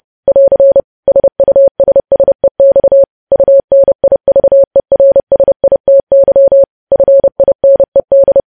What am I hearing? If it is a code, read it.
Morse code, heard as a keyed tone, about 30 wpm, 550 Hz. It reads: <AR> SUSSEX UNIVERSITY FINED
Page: news headlines in morse code at 30 wpm